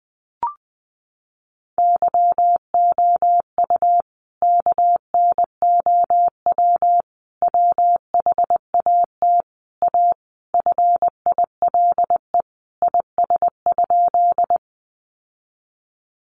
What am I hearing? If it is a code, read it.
Morse code, heard as a keyed tone, about 20 wpm, 700 Hz. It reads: YOU KNOW WHAT A FILE IS?
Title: Sentences from Top 1200 Words 20wpm